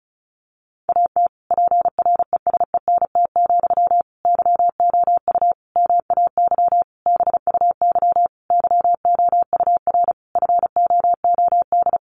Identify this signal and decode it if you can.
Morse code, heard as a keyed tone, about 34 wpm, 700 Hz. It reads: AT PRESENT, YOU MAY BUY YOUR FOOD
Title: Sentences from Top 800 Words 35wpm